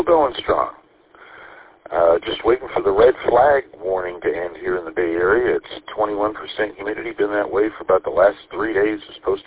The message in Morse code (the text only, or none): none